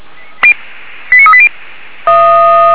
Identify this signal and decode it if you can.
Five-tone selective call (ZVEI), heard as quick runs of five tones, tones of about 70 ms. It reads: 89380